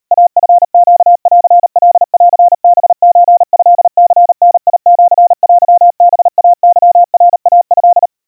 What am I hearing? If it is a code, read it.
Morse code (Japanese Wabun code), heard as a keyed tone, about 37 wpm, 700 Hz. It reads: イチネンカンハソトニダシテハイケナイト
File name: kawa / Morse sample / kamyu 140 cpm 700 Hz Crispy